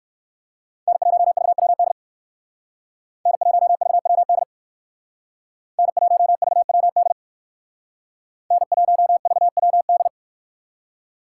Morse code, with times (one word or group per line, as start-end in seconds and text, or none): 0.9-1.9 N1VWD
3.3-4.4 N1VWD
5.8-7.1 N1VWD
8.5-10.1 N1VWD